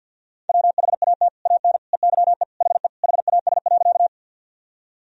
Morse code, heard as a keyed tone, about 51 wpm, 700 Hz. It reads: WHAT AN EYE HE HAS.
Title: Sentences from Top 1000 Words - Review Part 1 - 50wpm